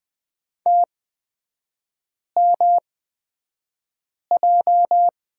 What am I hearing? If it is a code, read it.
Morse code, heard as a keyed tone, about 20 wpm, spaced as for 5 wpm, 700 Hz. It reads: TMJ
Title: 236 Lesson 38e - Letter q - Three Characters 20wpm